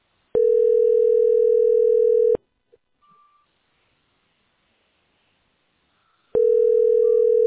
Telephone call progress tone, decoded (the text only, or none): ringback tone